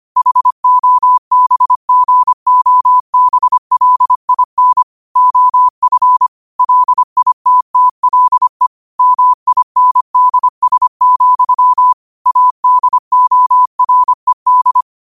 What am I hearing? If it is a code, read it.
Morse code, heard as a keyed tone, about 25 wpm, 1000 Hz. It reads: HOBGOBLIN OF LITTLE MINDS, ADORED
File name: qotdfiles / 25wpm